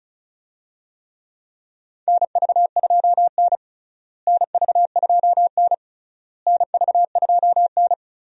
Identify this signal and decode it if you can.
Morse code, heard as a keyed tone, about 36 wpm, 700 Hz. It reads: NV2N NV2N NV2N
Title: USA Call Signs - 4 Characters - Repeated 3x - 35wpm